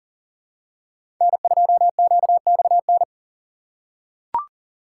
Morse code, heard as a keyed tone, about 38 wpm, 700 Hz. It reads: N2QXN
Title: USA Call Signs - 5 Characters - 4x Character Spacing - 40wpm